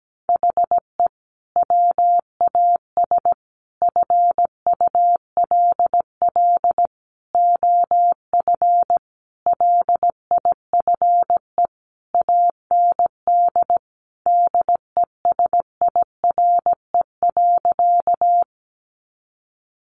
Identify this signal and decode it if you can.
Morse code, heard as a keyed tone, about 17 wpm, 700 Hz. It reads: HE WAS FULL OF LIFE AND DESIRE.
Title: Sentences from Top 1100 Words 17wpm